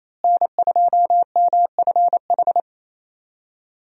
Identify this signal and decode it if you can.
Morse code, heard as a keyed tone, about 28 wpm, 700 Hz. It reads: N2MFH
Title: USA Call Signs - 5 Characters - Repeated 2x - 28wpm